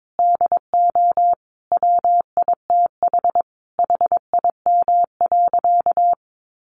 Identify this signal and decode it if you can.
Morse code, heard as a keyed tone, about 22 wpm, 700 Hz. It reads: DO WITH HIM.
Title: Sentences from Top 1000 Words 22wpm